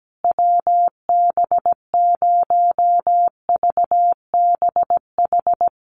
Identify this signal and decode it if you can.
Morse code, heard as a keyed tone, about 17 wpm, 700 Hz. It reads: WB0VBH
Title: USA Call Signs Repeated 3x 17wpm